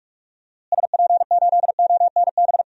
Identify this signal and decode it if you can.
Morse code, heard as a keyed tone, about 42 wpm, 700 Hz. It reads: SP8ONB
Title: DXCC-100 Call Signs - 6 Characters - Repeated 3x - 45wpm